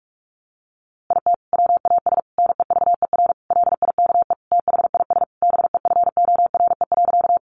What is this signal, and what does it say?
Morse code, heard as a keyed tone, about 46 wpm, 700 Hz. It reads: IT WAS NEVER LIKE THIS BEFORE.
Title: A Sound Of Thunder 45wpm